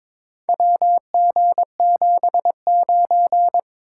Morse code, heard as a keyed tone, about 22 wpm, 700 Hz. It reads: WG79
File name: practice / 22wpm / Four Letter-Number Combinations - 22wpm